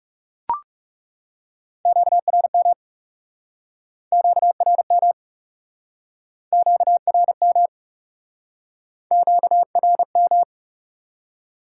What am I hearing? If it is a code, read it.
Morse code, heard as a keyed tone, about 38 wpm, 700 Hz. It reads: QRM QRM QRM QRM